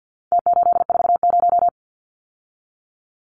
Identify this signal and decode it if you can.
Morse code, heard as a keyed tone, about 50 wpm, 700 Hz. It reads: T840
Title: Numbers - Contest Style - 4-Digit 50wpm